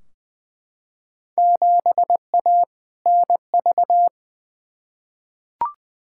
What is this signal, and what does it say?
Morse code, heard as a keyed tone, about 21 wpm, 700 Hz. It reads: 7A NV
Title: Field Day Exchanges Encore - 20wpm